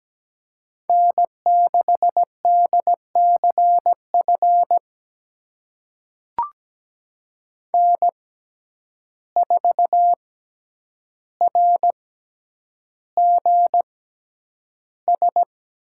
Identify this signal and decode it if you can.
Morse code, heard as a keyed tone, about 17 wpm, spaced as for 9 wpm, 700 Hz. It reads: N6DCF N4RGS
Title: USA Call Signs - 5 Characters - 6x Character Spacing - 17wpm